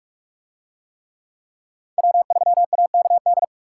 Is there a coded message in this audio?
Yes: Morse code from 2.0 to 3.4 s, heard as a keyed tone, about 44 wpm, 700 Hz. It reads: W3AKD